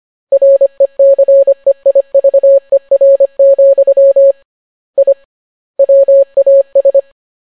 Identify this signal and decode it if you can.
Morse code, heard as a keyed tone, about 25 wpm, 550 Hz. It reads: RECEIVER, I WAS